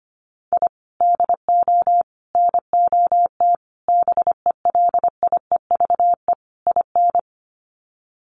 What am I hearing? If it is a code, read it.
Morse code, heard as a keyed tone, about 25 wpm, 700 Hz. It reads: I DO NOT BELIEVE IN
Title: Sets of 5 Words Encore 25wpm